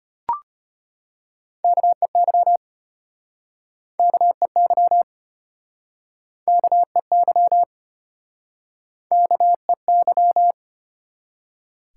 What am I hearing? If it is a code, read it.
Morse code, heard as a keyed tone, about 33 wpm, 700 Hz. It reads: KEY KEY KEY KEY